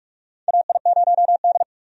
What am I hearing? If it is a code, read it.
Morse code, heard as a keyed tone, about 47 wpm, 700 Hz. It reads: AI0D
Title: USA Call Signs - 4 Characters - 45wpm